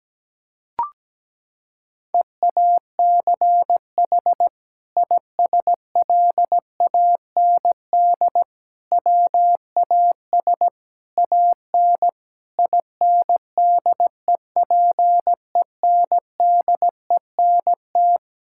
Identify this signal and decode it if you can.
Morse code, heard as a keyed tone, about 17 wpm, 700 Hz. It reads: EACH ISLAND WAS AN INDEPENDENT
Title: Sentences from Top 1000 Words - Review Part 6 - 17wpm